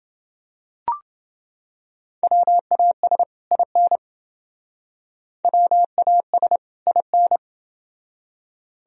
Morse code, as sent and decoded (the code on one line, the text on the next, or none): .-- .- ... / .. -. / .-- .- ... / .. -.
WAS IN WAS IN